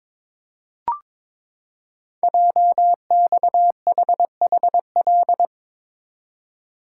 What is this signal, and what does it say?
Morse code, heard as a keyed tone, about 22 wpm, 700 Hz. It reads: JXHHL
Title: Code Groups - 5 Letters - 22wpm